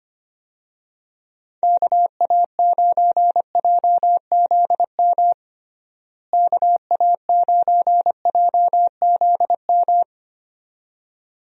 Morse code, heard as a keyed tone, about 25 wpm, 700 Hz. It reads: KA9JZM KA9JZM
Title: USA Call Signs - 6 Characters - Repeated 2x - 25wpm